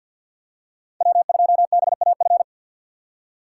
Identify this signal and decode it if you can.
Morse code, heard as a keyed tone, about 47 wpm, 690 Hz. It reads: W2BAF
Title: USA Call Signs - 2x Character Spacing - 50wpm